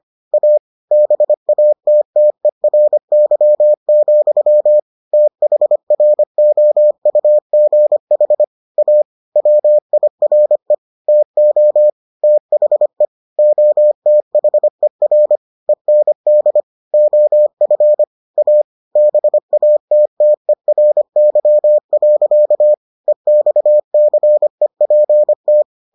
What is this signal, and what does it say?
Morse code, heard as a keyed tone, about 25 wpm, 600 Hz. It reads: A BATTERY, THROUGH A WIRE TO THE OTHER END OF A BATTERY. EXCEPT